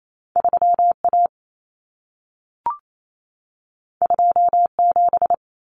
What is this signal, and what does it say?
Morse code, heard as a keyed tone, about 28 wpm, 700 Hz. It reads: 3A 27
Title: Common Cut Numbers - 2-Digit 28wpm